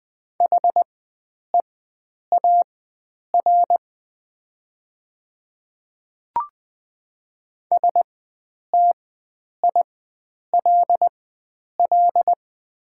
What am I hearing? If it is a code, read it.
Morse code, heard as a keyed tone, about 20 wpm, spaced as for 9 wpm, 700 Hz. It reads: HEAR STILL
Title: 4x Character Spacing - Top 300 Words - 20wpm